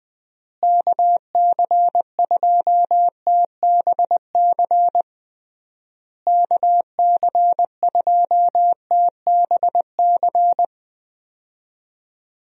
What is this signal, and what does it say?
Morse code, heard as a keyed tone, about 20 wpm, 700 Hz. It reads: KC2TBC KC2TBC
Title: USA Call Signs - 6 Characters - Repeated 3x - 20wpm